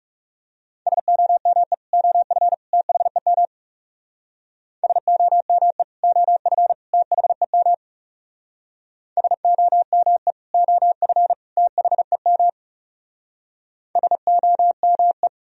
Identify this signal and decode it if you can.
Morse code, heard as a keyed tone, about 36 wpm, 700 Hz. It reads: SOME OF THEM SOME OF THEM SOME OF THEM SOME
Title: Speed-Racing - Sets of 3 Words 30wpm